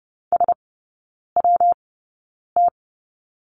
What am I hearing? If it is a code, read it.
Morse code, heard as a keyed tone, about 30 wpm, spaced as for 9 wpm, 700 Hz. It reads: SWT
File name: Lesson 19e - Letter w - Three Characters 30wpm